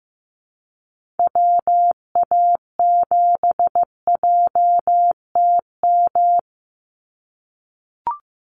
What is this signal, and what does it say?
Morse code, heard as a keyed tone, about 15 wpm, 700 Hz. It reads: WA7JTM
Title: NA SOTA Activator Callsigns 15wpm